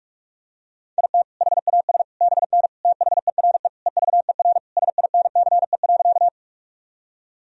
Morse code, heard as a keyed tone, about 41 wpm, 700 Hz. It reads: IT HAS BN THERE EVER SINCE.
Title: Sentences from Top 400 Words and Abbreviations 45wpm